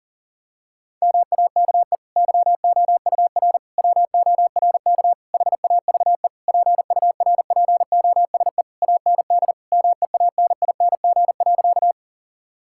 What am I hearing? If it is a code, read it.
Morse code, heard as a keyed tone, about 37 wpm, 700 Hz. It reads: MAKE YOUR WORK HAVE PURPOSE AND MEANING.
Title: Sentences from Top 1000 Words - Review Part 6 - 40wpm